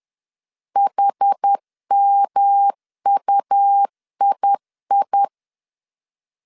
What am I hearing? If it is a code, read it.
Morse code, heard as a keyed tone, about 10 wpm, 800 Hz. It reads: HMUII